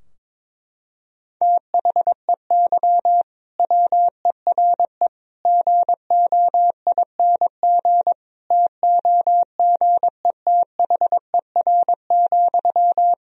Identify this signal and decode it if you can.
Morse code, heard as a keyed tone, about 22 wpm, 700 Hz. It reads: THEY WERE GOING TOGETHER,
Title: Sentences from Top 1000 Words - Review Part 4 - 22wpm